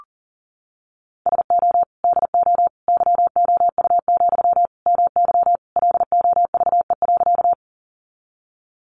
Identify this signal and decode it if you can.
Morse code, heard as a keyed tone, about 40 wpm, 700 Hz. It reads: SO DO YOU, MY LOVE.
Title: Sentences from Top 400 Words 40wpm